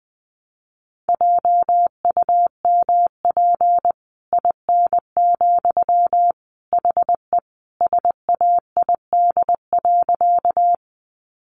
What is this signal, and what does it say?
Morse code, heard as a keyed tone, about 20 wpm, 700 Hz. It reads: JUMP IN, HE SAID.
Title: Sentences from Top 1300 Words 20wpm